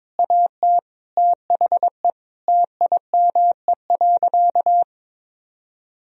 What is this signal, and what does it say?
Morse code, heard as a keyed tone, about 22 wpm, 700 Hz. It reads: AT THE TIME.
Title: Sentences from Top 1000 Words - Review Part 4 - 22wpm